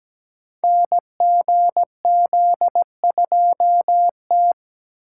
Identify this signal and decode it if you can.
Morse code, heard as a keyed tone, about 17 wpm, 700 Hz. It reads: NGZ2T
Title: Code Groups - 5 Letter:Numbers - 17wpm